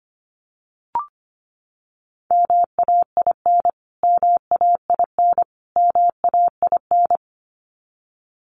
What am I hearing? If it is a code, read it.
Morse code, heard as a keyed tone, about 25 wpm, 700 Hz. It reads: MAIN MAIN MAIN
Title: Top 500 Words Repeated 25wpm